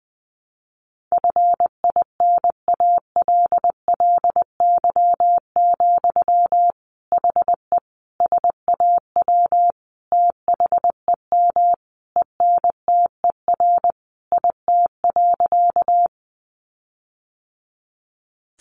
Morse code, heard as a keyed tone, about 20 wpm, 700 Hz. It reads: FINALLY, HE SAW THEM ENTER IT.
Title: Sentences from Top 1300 Words 20wpm